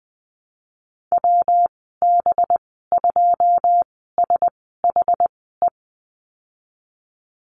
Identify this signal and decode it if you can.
Morse code, heard as a keyed tone, about 19 wpm, spaced as for 14 wpm, 700 Hz. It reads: WB2SHE
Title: 186 Lesson 30g - Letter b - Call Signs 20wpm